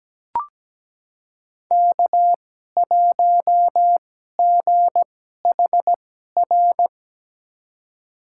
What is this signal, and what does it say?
Morse code, heard as a keyed tone, about 17 wpm, spaced as for 12 wpm, 700 Hz. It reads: K1GHR
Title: USA Call Signs - 5 Characters - 2x Character Spacing - 17wpm